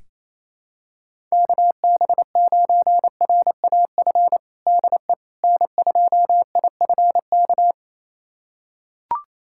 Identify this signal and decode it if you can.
Morse code, heard as a keyed tone, about 28 wpm, 700 Hz. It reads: KB9RAF DE N2IFK
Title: USA Callsign DE Callsign 28wpm